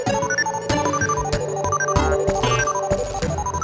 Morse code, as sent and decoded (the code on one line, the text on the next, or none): none